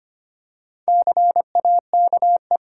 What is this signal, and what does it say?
Morse code, heard as a keyed tone, about 25 wpm, 700 Hz. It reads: CAKE
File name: practice / 25wpm / Easy Words with Hints - 25wpm